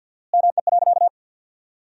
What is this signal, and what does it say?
Morse code, heard as a keyed tone, about 52 wpm, 700 Hz. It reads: ME.
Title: 4x Word-Spacing - Sentences from Top 500 Words - 50wpm